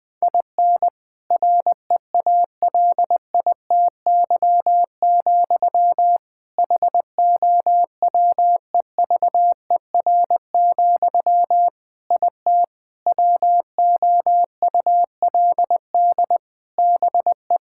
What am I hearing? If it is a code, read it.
Morse code, heard as a keyed tone, about 20 wpm, 700 Hz. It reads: IN REALITY, HOWEVER, IT WOULD BE
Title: Sentences from Top 1600 Words 20wpm